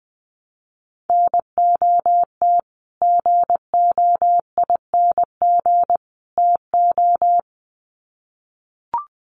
Morse code, heard as a keyed tone, about 20 wpm, 700 Hz. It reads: NOT GOING TO
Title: Sets of 3 Words 20wpm